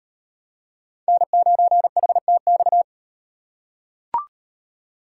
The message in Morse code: N9HTX

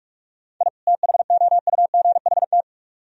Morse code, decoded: I THOUGHT

